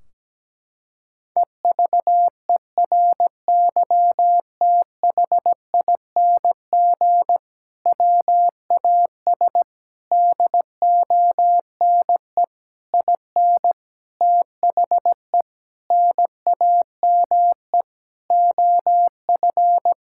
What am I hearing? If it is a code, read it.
Morse code, heard as a keyed tone, about 17 wpm, 700 Hz. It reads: EVERYTHING WAS DONE IN THE NAME OF
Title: Sentences from Top 1600 Words 17wpm